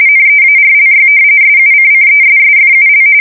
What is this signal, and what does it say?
RTTY, about 45.45 baud. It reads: NEARLY SUCCESSFUL
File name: fsk170 / usnavy4